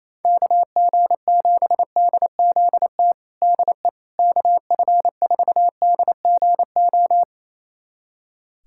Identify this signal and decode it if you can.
Morse code, heard as a keyed tone, about 28 wpm, 700 Hz. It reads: KG7DZT DE KF4DGO
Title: USA Callsign DE Callsign - 6 Characters - 28wpm